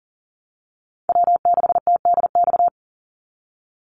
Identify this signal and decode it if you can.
Morse code, heard as a keyed tone, about 40 wpm, 700 Hz. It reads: W6TDX
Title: NA SOTA Chaser Callsigns 40wpm